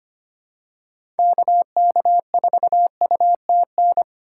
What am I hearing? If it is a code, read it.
Morse code, heard as a keyed tone, about 25 wpm, 700 Hz. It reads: KK4UTN
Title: USA Call Signs - 6 Characters - 4x Character Spacing - 25wpm